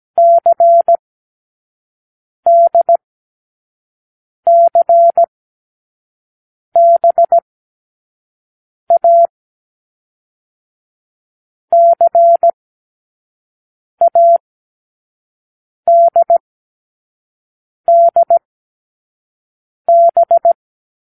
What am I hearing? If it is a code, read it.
Morse code, heard as a keyed tone, about 17 wpm, spaced as for 5 wpm, 680 Hz. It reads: CDCBA CADDB
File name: k7qo / Lesson6(A-D)